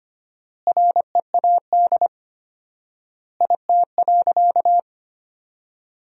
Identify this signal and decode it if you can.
Morse code, heard as a keyed tone, about 25 wpm, 700 Hz. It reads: READ IT.
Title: 4x Word-Spacing - Sentences from Top 500 Words - 25wpm